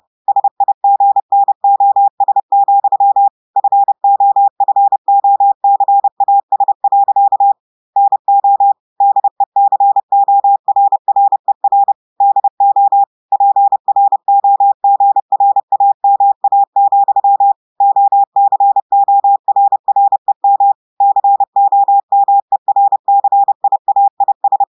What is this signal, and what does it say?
Morse code, heard as a keyed tone, about 30 wpm, 800 Hz. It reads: SIGNOS, FOFOCAS. NO DECORRER DO PROGRAMA, OCORREM COMERCIAIS
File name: audios / portugais / portugais texte en clair 2 vitesse 30 mots minutes